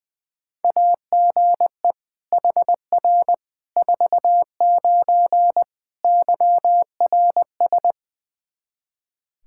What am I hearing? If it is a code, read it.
Morse code, heard as a keyed tone, about 21 wpm, 700 Hz. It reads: AGE HR 49 YRS
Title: K1USN SST QSO Practice 20wpm